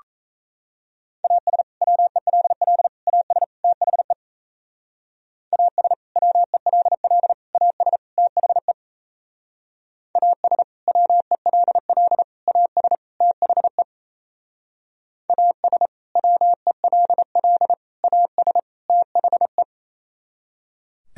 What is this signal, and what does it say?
Morse code, heard as a keyed tone, about 34 wpm, 700 Hz. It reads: AS WELL AS THE AS WELL AS THE AS WELL AS THE AS WELL AS THE
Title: Speed-Racing - Sets of 4 Words 28wpm